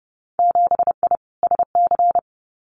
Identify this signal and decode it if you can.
Morse code, heard as a keyed tone, about 30 wpm, 700 Hz. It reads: 7I SC
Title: Winter Field Day Exchanges Encore 30wpm